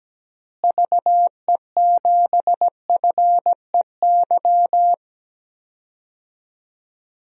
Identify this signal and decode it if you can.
Morse code, heard as a keyed tone, about 17 wpm, 700 Hz. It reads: VE7FEY
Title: DXCC-100 Call Signs - 6 Characters - Repeated 3x - 17wpm